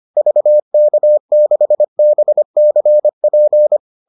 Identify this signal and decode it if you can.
Morse code, heard as a keyed tone, about 25 wpm, 600 Hz. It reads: VK6BCP